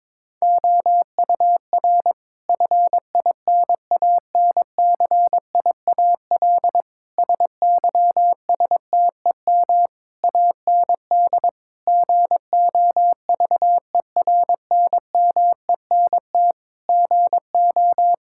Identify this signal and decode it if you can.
Morse code, heard as a keyed tone, about 22 wpm, 700 Hz. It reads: OUR FINANCIAL SYSTEM AND GOVERNMENT GO
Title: Sentences from Top 1000 Words - Review Part 5 - 22wpm